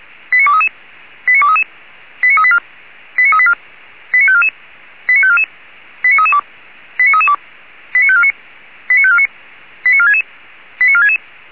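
Five-tone selective call (ZVEI), heard as quick runs of five tones, tones of about 70 ms. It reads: 89233 89233 89384 89384 89544 89544 89332 89332 89549 89549 89466 89466
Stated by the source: ZVEI selective call received by SDR, logged by the Feuerwehr Orlamünde (fire brigade) audio server.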